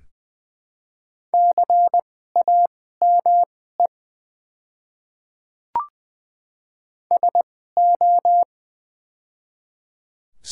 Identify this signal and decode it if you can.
Morse code, heard as a keyed tone, about 20 wpm, spaced as for 14 wpm, 700 Hz. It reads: CAME SO
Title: 2x Character Spacing - Top 200 Words - 20wpm